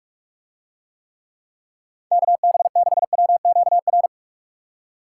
Morse code, heard as a keyed tone, about 49 wpm, 700 Hz. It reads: KB6WQR